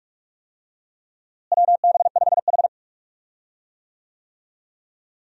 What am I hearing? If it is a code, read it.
Morse code, heard as a keyed tone, about 42 wpm, 700 Hz. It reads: WB5H
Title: USA Call Signs - 4 Characters - Repeated 3x - 45wpm